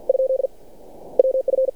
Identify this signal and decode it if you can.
Morse code, heard as a keyed tone, about 49 wpm, 530 Hz. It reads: ? WF
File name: archived / CRSnet